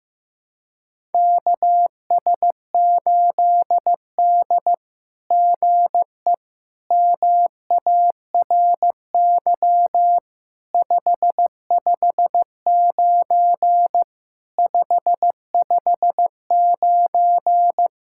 Morse code, heard as a keyed tone, about 15 wpm, 700 Hz. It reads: KS8D GE MARY 559 559
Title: SOTA QSO Practice 15wpm